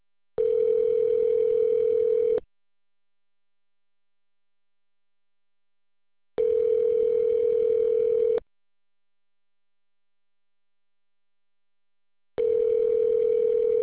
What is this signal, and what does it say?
Ringback tone (North America), 440 + 480 Hz.